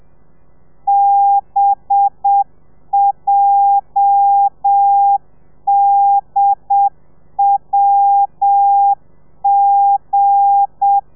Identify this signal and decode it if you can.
Morse code, heard as a keyed tone, about 7 wpm, 800 Hz. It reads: BJDWG
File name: telegrafia / i2tfj / Lez8 30